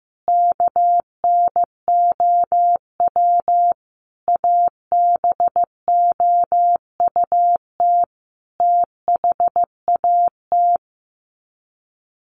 Morse code, heard as a keyed tone, about 15 wpm, 700 Hz. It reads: KNOW ABOUT THAT
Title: Sets of 5 Words Encore2 15wpm